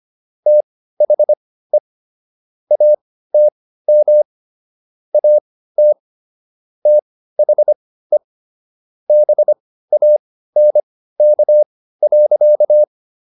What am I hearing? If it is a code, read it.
Morse code, heard as a keyed tone, about 25 wpm, spaced as for 15 wpm, 600 Hz. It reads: THE ATM AT THE BANK.